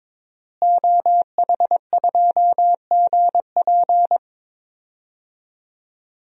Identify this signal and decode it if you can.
Morse code, heard as a keyed tone, about 22 wpm, 700 Hz. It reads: OH2GP
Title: DXCC-100 Call Signs - 5 Characters - Repeated 2x - 22wpm